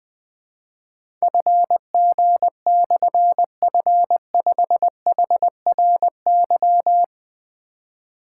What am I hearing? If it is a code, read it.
Morse code, heard as a keyed tone, about 20 wpm, 700 Hz. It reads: FG/F5HRY
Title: Difficult Call Signs 20wpm